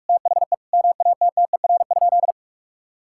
Morse code, heard as a keyed tone, about 42 wpm, 700 Hz. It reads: THE MATTER?
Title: Sentences from Top 1000 Words - Review Part 6 - 45wpm